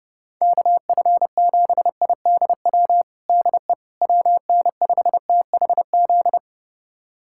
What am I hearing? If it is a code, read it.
Morse code, heard as a keyed tone, about 30 wpm, 700 Hz. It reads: KF7IDW DE WN5THZ
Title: USA Callsign DE Callsign - 6 Characters - 30wpm